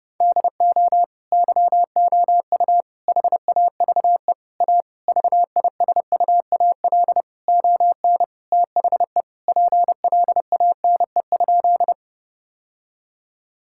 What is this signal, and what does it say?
Morse code, heard as a keyed tone, about 30 wpm, 700 Hz. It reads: DO YOU HAVE A VISUAL ON THE PLANE?
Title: Sentences from Top 1000 Words - Review Part 6 - 30wpm